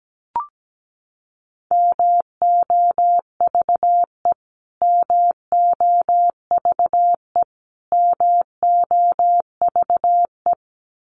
Morse code, heard as a keyed tone, about 17 wpm, 700 Hz. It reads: MOVE MOVE MOVE
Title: Top 500 Words Repeated 17wpm